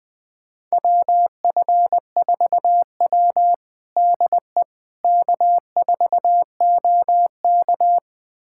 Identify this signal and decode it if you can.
Morse code, heard as a keyed tone, about 21 wpm, 700 Hz. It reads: WF4W DE K4OK
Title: USA Callsign DE Callsign - 4 Characters - 20wpm